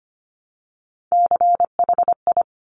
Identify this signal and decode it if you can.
Morse code, heard as a keyed tone, about 25 wpm, 700 Hz. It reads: CHI
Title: Top Three Letter Combinations in Words 25wpm